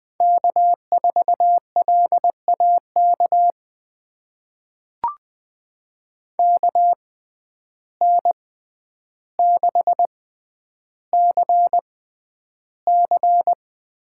Morse code, heard as a keyed tone, about 20 wpm, spaced as for 10 wpm, 700 Hz. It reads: K4LAK KN6CC
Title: USA Call Signs - 5 Characters - 6x Character Spacing - 20wpm